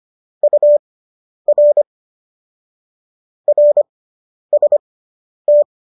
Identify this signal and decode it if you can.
Morse code, heard as a keyed tone, about 25 wpm, spaced as for 10 wpm, 590 Hz. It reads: UR RST
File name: qso005 10